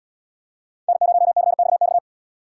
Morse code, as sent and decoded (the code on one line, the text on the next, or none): -. ----- -.-. -..- -..-
N0CXX